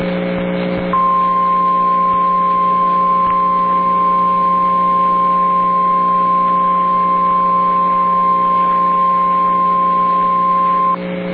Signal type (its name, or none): Emergency Alert System (SAME) tones